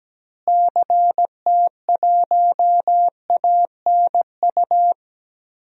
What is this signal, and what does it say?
Morse code, heard as a keyed tone, about 17 wpm, 700 Hz. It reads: CT1ANU
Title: DXCC-100 Call Signs - 6 Characters - Repeated 3x - 17wpm